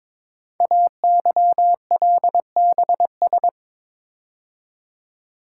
Morse code, heard as a keyed tone, about 22 wpm, 700 Hz. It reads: AYLBS
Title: Code Groups - 5 Letter:Numbers - 22wpm